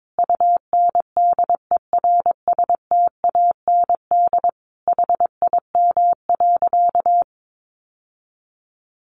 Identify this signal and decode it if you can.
Morse code, heard as a keyed tone, about 22 wpm, 700 Hz. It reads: UNDERSTAND HIM.
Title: Sentences from Top 600 Words 22wpm